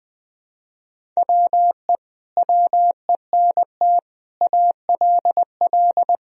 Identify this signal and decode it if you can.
Morse code, heard as a keyed tone, about 20 wpm, 700 Hz. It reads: WE WENT ALL
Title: Sentences from Top 1000 Words - Review Part 1 - 20wpm